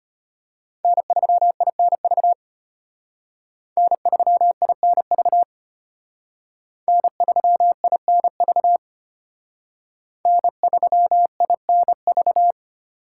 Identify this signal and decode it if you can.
Morse code, heard as a keyed tone, about 32 wpm, 700 Hz. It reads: N3INV N3INV N3INV N3INV